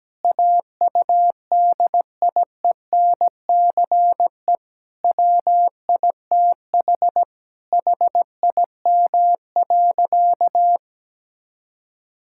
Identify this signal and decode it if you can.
Morse code, heard as a keyed tone, about 17 wpm, 700 Hz. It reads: AUDIENCE WITH HIM.
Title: Sentences from Top 1000 Words - Review Part 5 - 17wpm